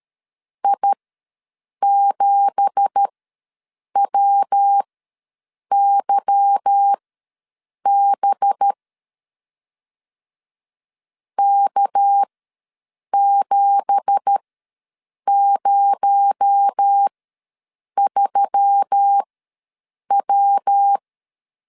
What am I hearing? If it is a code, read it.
Morse code, heard as a keyed tone, about 13 wpm, spaced as for 7 wpm, 800 Hz. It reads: I7WYB K703W
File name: LEZIONI / 9 2867 ws 12wpm 3431